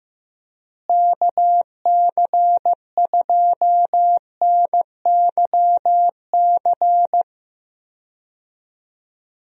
Morse code, heard as a keyed tone, about 15 wpm, 700 Hz. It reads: KC2NYC
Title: USA Call Signs - 6 Characters - Repeated 3x - 15wpm